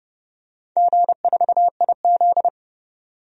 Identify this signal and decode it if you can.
Morse code, heard as a keyed tone, about 30 wpm, 700 Hz. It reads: G4IZ
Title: DXCC-100 Call Signs - 4 Characters - 30wpm